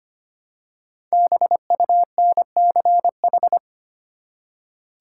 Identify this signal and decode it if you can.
Morse code, heard as a keyed tone, about 25 wpm, 700 Hz. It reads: BUNCH
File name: 5 Letter Words - First 3 Letters Given - 25wpm